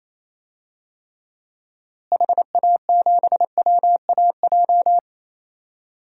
Morse code, heard as a keyed tone, about 28 wpm, 700 Hz. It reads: HA7WAJ
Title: DXCC-100 Call Signs - 6 Characters - Repeated 2x - 28wpm